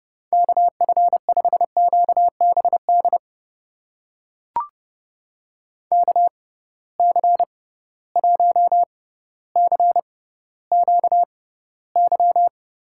Morse code, heard as a keyed tone, about 30 wpm, spaced as for 15 wpm, 700 Hz. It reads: KF5QBD KC1CQY
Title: USA Call Signs - 6x Character Spacing - 30wpm